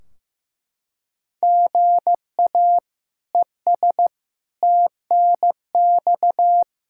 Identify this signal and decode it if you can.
Morse code, heard as a keyed tone, about 16 wpm, 700 Hz. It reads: GA ES TNX
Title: Practice QSO - Exchange 1 - Beginning - 15wpm